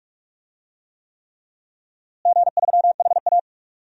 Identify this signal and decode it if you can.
Morse code, heard as a keyed tone, about 46 wpm, 700 Hz. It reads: G3HA
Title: DXCC-100 Call Signs - 4 Characters - Repeated 3x - 45wpm